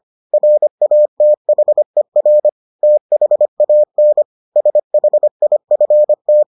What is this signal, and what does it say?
Morse code, heard as a keyed tone, about 25 wpm, 600 Hz. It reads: RATHER THAN SHIFT